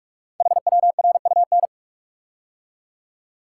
Morse code, heard as a keyed tone, about 44 wpm, 700 Hz. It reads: HWRUN
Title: Code Groups - 5 Letters - 45wpm